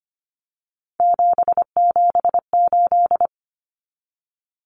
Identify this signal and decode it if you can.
Morse code, heard as a keyed tone, about 25 wpm, 700 Hz. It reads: 778
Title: Numbers - 3-Digit 25wpm